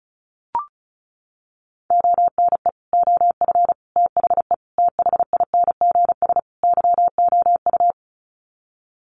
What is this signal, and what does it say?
Morse code, heard as a keyed tone, about 34 wpm, 700 Hz. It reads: ONE OF THE THINGS YOU
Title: Sets of 5 Words Encore2 35wpm